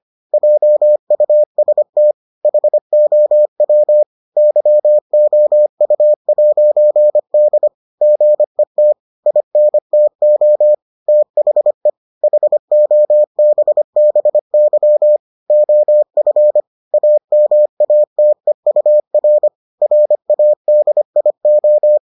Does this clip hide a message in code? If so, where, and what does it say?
Yes: Morse code from 0.3 to 22.0 s, heard as a keyed tone, about 25 wpm, 600 Hz. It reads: JUST HOW YOU'D GET INTO THE HOBBY OF AMATEUR RADIO